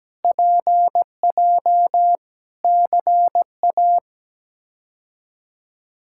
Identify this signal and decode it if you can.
Morse code, heard as a keyed tone, about 17 wpm, 700 Hz. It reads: PJ CA
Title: K1USN SST QSO Practice - 17wpm